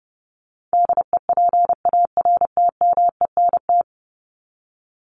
Morse code, heard as a keyed tone, about 30 wpm, 700 Hz. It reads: DEPARTMENT
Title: Top 751-1000 Words 30wpm